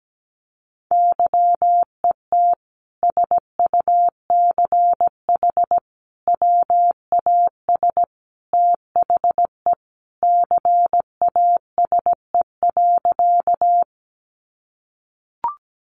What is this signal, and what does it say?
Morse code, heard as a keyed tone, about 17 wpm, 700 Hz. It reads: YET SUCH WAS THE CASE.
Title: Sentences from Top 700 Words 17wpm